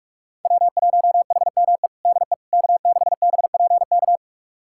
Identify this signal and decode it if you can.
Morse code, heard as a keyed tone, about 46 wpm, 700 Hz. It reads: W1HME DE K6BPK
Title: USA Callsign DE Callsign - 5 Characters - 45wpm